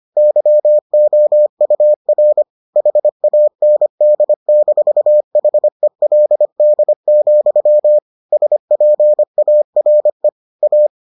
Morse code, heard as a keyed tone, about 25 wpm, 600 Hz. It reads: YOUR HAND-HELD, SPARE A